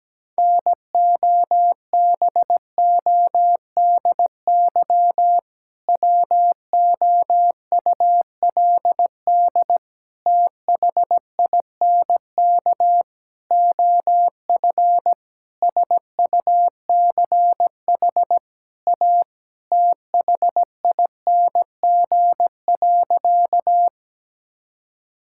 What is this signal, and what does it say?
Morse code, heard as a keyed tone, about 17 wpm, 700 Hz. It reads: NOBODY WOULD THINK OF SUCH A THING.
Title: Sentences from Top 1600 Words 17wpm